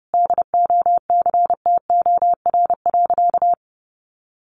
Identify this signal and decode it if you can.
Morse code, heard as a keyed tone, about 30 wpm, 700 Hz. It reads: DOCTOR.
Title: Sentences from Top 1500 Words 30wpm